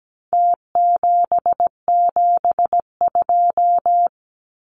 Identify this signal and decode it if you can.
Morse code, heard as a keyed tone, about 17 wpm, 700 Hz. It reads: T772
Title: Numbers - Contest Style - 4-Digit 17wpm